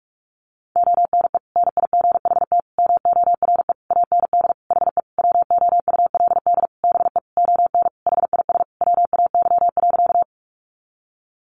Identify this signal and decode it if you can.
Morse code, heard as a keyed tone, about 45 wpm, 700 Hz. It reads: ONE NIGHT MORE AND HE WOULD BE ON HIS WAY.
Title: Sentences from Top 300 Words 45wpm